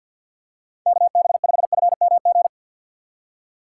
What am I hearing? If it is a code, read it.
Morse code, heard as a keyed tone, about 53 wpm, 690 Hz. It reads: KB5FMG
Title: USA Call Signs - 6x Character Spacing - 50wpm